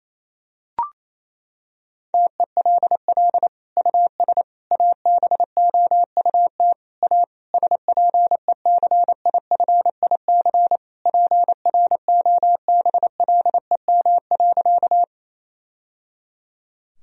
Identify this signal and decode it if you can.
Morse code, heard as a keyed tone, about 28 wpm, 700 Hz. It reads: TELL US ABOUT A SPECIFIC PROBLEM.
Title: Sentences from Top 1000 Words - Review Part 3 - 28wpm